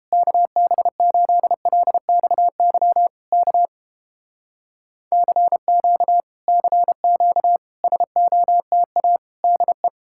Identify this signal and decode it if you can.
Morse code, heard as a keyed tone, about 31 wpm, 700 Hz. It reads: KB8LXY K CQ CQ SOTA DE